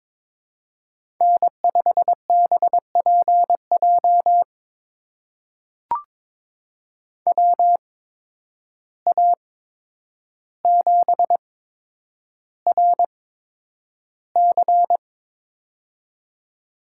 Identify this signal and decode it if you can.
Morse code, heard as a keyed tone, about 22 wpm, spaced as for 9 wpm, 700 Hz. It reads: N5BPJ WA7RC
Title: USA Call Signs - 5 Characters - 8x Character Spacing - 22wpm